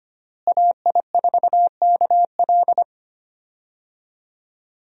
Morse code, heard as a keyed tone, about 25 wpm, 700 Hz. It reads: AI4KL
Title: USA Call Signs - 5 Characters - Repeated 2x - 25wpm